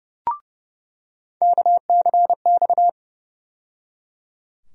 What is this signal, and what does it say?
Morse code, heard as a keyed tone, about 32 wpm, 700 Hz. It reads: KCX
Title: USA Call Sign Postfixes - 3 Letters 30wpm